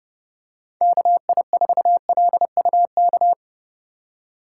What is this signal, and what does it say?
Morse code, heard as a keyed tone, about 31 wpm, 700 Hz. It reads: KI4LUK
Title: USA Call Signs - 3x Character Spacing - 30wpm